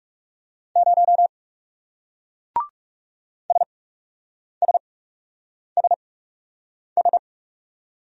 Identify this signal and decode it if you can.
Morse code, heard as a keyed tone, about 34 wpm, spaced as for 8 wpm, 700 Hz. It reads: 0 SSSS
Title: Speed-Racing Single Letter-Number 30wpm